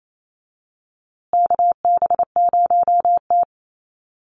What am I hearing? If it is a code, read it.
Morse code, heard as a keyed tone, about 28 wpm, 700 Hz. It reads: KB0T
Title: USA Call Signs Repeated 3x 28wpm